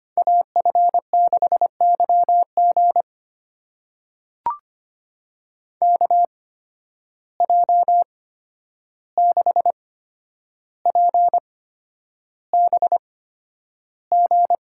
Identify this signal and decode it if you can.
Morse code, heard as a keyed tone, about 25 wpm, spaced as for 10 wpm, 700 Hz. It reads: AF6YG KJ6PBG
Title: USA Call Signs - 8x Character Spacing - 25wpm